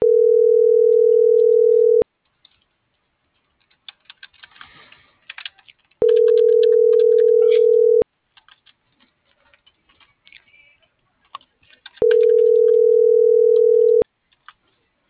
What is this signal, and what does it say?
Ringback tone (North America), 440 + 480 Hz.